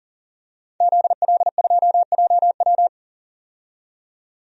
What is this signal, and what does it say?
Morse code, heard as a keyed tone, about 40 wpm, 700 Hz. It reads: ZL2JW
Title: DXCC-100 Call Signs - 5 Characters - 40wpm